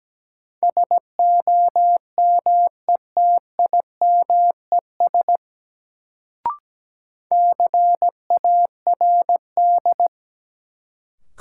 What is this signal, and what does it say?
Morse code, heard as a keyed tone, about 17 wpm, 700 Hz. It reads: SOMETIMES CARD
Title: Sentences from Top 600 Words - New Word Drill - 17wpm